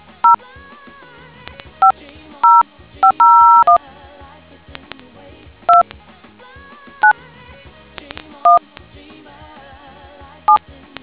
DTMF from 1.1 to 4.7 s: 50501